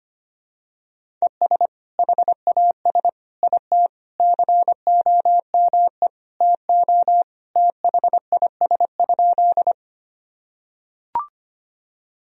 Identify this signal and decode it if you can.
Morse code, heard as a keyed tone, about 25 wpm, 700 Hz. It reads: ES HAS IT COME TO THIS?
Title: Sentences from Top 100 Words and Abbreviations 25wpm